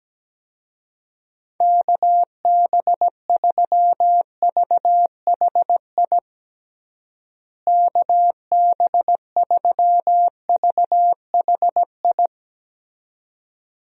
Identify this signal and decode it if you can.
Morse code, heard as a keyed tone, about 17 wpm, 700 Hz. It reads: KB3VHI KB3VHI
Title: USA Call Signs - 6 Characters - Repeated 2x - 17wpm